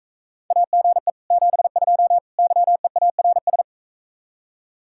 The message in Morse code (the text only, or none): AGE 72 YEARS